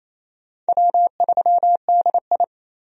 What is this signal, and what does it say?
Morse code, heard as a keyed tone, about 28 wpm, 700 Hz. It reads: W3DI
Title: USA Call Signs - 4 Characters - 28wpm